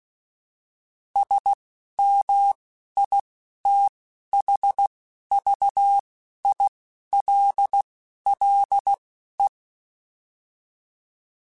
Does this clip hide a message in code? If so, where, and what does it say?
Yes: Morse code from 1.2 to 9.5 s, heard as a keyed tone, about 16 wpm, spaced as for 11 wpm, 800 Hz. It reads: SMITHVILLE